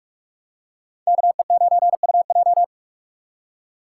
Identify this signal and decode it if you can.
Morse code, heard as a keyed tone, about 42 wpm, 700 Hz. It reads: KE9UJ